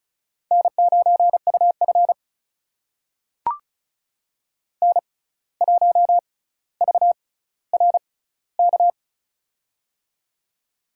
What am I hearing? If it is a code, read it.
Morse code, heard as a keyed tone, about 36 wpm, spaced as for 17 wpm, 700 Hz. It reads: N9UF N1VRK
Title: USA Call Signs - 6x Character Spacing - 35wpm